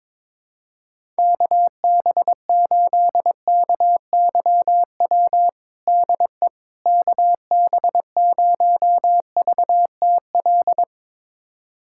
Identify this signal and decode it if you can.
Morse code, heard as a keyed tone, about 22 wpm, 700 Hz. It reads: KB8KYW DE KB0VTL
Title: USA Callsign DE Callsign 22wpm